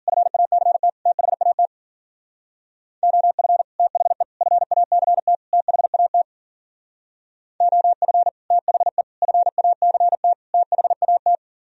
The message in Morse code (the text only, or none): FACT THAT OF THE FACT THAT OF THE FACT THAT